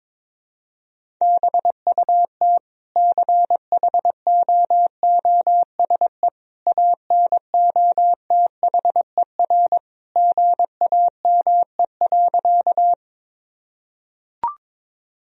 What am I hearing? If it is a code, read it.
Morse code, heard as a keyed tone, about 22 wpm, 700 Hz. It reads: BUT CHOOSE ANOTHER GAME.
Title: Sentences from Top 1000 Words - Review Part 4 - 22wpm